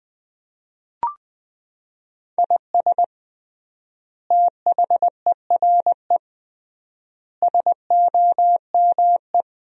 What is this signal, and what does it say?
Morse code, heard as a keyed tone, about 20 wpm, 700 Hz. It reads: IS THERE SOME
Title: 3x Word-Spacing - Sentences from Top 200 Words - 20wpm